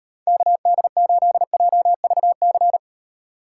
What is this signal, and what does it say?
Morse code, heard as a keyed tone, about 38 wpm, 700 Hz. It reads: KD8JVC